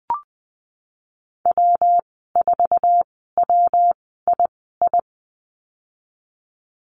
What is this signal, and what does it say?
Morse code, heard as a keyed tone, about 20 wpm, spaced as for 15 wpm, 700 Hz. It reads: W4WII
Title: 122 Lesson 20g - Number 3 - Call Signs 20wpm